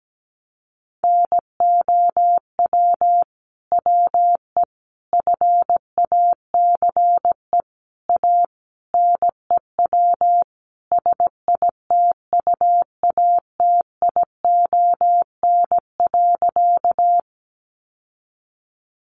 Morse code, heard as a keyed tone, about 17 wpm, 700 Hz. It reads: NOW WE FACE A NEW SITUATION.
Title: Sentences from Top 1200 Words 17wpm